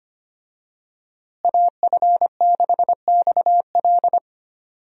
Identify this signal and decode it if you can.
Morse code, heard as a keyed tone, about 25 wpm, 700 Hz. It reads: AF6XL
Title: USA Call Signs - 5 Characters - Repeated 3x - 25wpm